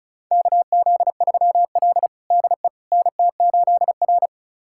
Morse code, heard as a keyed tone, about 34 wpm, 700 Hz. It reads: KZ3L DE NT8R
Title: USA Callsign DE Callsign - 4 Characters - 35wpm